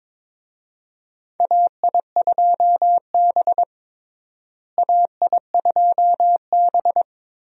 Morse code, heard as a keyed tone, about 22 wpm, 700 Hz. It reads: AI2B AI2B
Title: USA Call Signs - 4 Characters - Repeated 2x - 22wpm